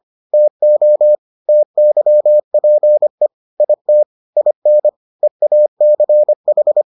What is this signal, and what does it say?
Morse code, heard as a keyed tone, about 25 wpm, 600 Hz. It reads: TO TYPE IT IN EACH